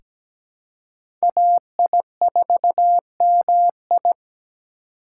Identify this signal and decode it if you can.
Morse code, heard as a keyed tone, about 17 wpm, 700 Hz. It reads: AI4MI
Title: K1USN SST QSO Practice - 17wpm